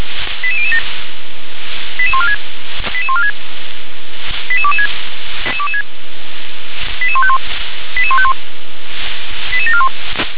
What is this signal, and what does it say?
Five-tone selective call (ZVEI), heard as quick runs of five tones, tones of about 70 ms. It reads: 88007 88146 88146 88226 88151 88151 88741
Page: SDR/ZVEI